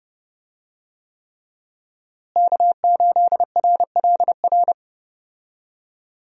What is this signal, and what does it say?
Morse code, heard as a keyed tone, about 30 wpm, 700 Hz. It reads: K8RLR
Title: USA Call Signs - 5 Characters - Repeated 3x - 30wpm